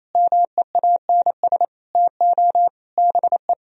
Morse code, heard as a keyed tone, about 37 wpm, 700 Hz. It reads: MEANS TO BE